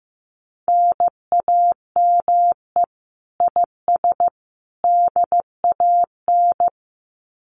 Morse code, heard as a keyed tone, about 15 wpm, 700 Hz. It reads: NAME IS DAN
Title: Common QSO Phrases 15wpm